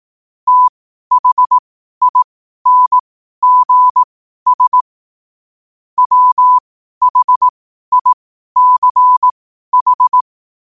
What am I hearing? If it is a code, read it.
Morse code, heard as a keyed tone, about 18 wpm, spaced as for 12 wpm, 1000 Hz. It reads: THINGS WHICH
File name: qotdfiles / 12wpm